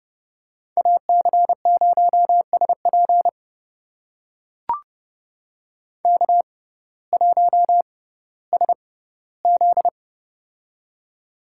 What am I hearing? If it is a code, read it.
Morse code, heard as a keyed tone, about 30 wpm, spaced as for 16 wpm, 700 Hz. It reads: AC0SP K1SZ
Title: USA Call Signs - 6x Character Spacing - 30wpm